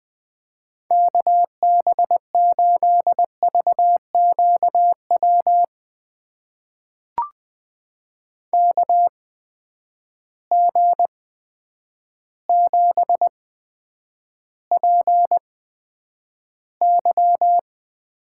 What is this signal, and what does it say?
Morse code, heard as a keyed tone, about 20 wpm, spaced as for 9 wpm, 700 Hz. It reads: KB8VQW KG7PY
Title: USA Call Signs - 6 Characters - 8x Character Spacing - 20wpm